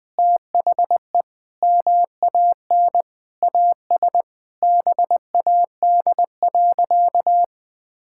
Morse code, heard as a keyed tone, about 20 wpm, 700 Hz. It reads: THE MAN AS BAD.
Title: Sentences from Top 1000 Words - Review Part 6 - 20wpm